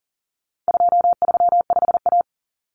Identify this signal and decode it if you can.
Morse code, heard as a keyed tone, about 40 wpm, 700 Hz. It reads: 235A
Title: Common Cut Numbers - 4-Digit 40wpm